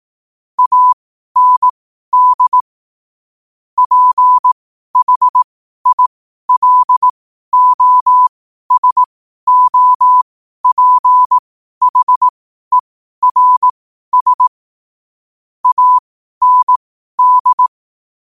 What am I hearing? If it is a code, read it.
Morse code, heard as a keyed tone, about 18 wpm, spaced as for 12 wpm, 1000 Hz. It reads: AND PHILOSOPHERS AND
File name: qotdfiles / 12wpm